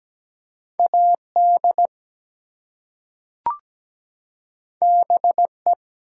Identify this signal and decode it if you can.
Morse code, heard as a keyed tone, about 17 wpm, 700 Hz. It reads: AD BE
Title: Two Letter Words - 17wpm